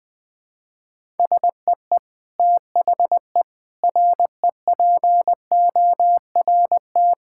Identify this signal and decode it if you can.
Morse code, heard as a keyed tone, about 20 wpm, 700 Hz. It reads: SEE THE REPORT
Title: Sentences from Top 1000 Words - Review Part 6 - 20wpm